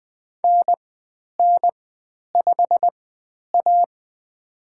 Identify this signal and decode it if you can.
Morse code, heard as a keyed tone, about 20 wpm, spaced as for 10 wpm, 700 Hz. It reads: NN5A
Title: 095 Lesson 16g - Letter c - Call Signs 20wpm